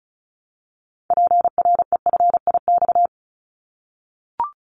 Morse code, heard as a keyed tone, about 34 wpm, 700 Hz. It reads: PREFIX
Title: Callsign Prefixes for Top 100 DXCC - 35wpm